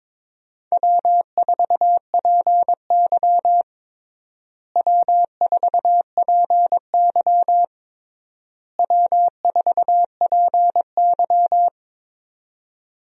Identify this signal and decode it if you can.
Morse code, heard as a keyed tone, about 22 wpm, 700 Hz. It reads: W4PY W4PY W4PY
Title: USA Call Signs - 4 Characters - Repeated 3x - 22wpm